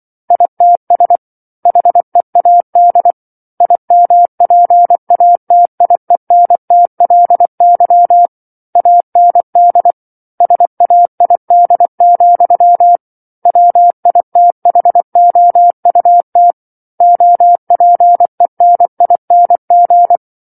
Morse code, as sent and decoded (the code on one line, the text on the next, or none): .. - ... / .... .- -.. / .. -- .--. .- - .. -. - .-.. -.-- / .- -. -.. / ... .- .. -.. --..-- / .-- .. - .... --- ..- - / --- .--. -. .. -. --.
ITS HEAD IMPATIENTLY AND SAID, WITHOUT OPENING